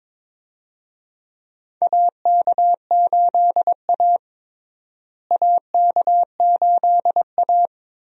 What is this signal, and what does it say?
Morse code, heard as a keyed tone, about 21 wpm, 700 Hz. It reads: AK8A AK8A
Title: USA Call Signs - 4 Characters - Repeated 3x - 22wpm